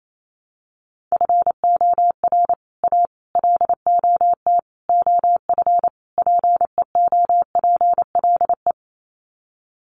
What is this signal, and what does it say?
Morse code, heard as a keyed tone, about 28 wpm, 700 Hz. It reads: FOR A LOT OF PEOPLE
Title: Sets of 5 Words Encore 28wpm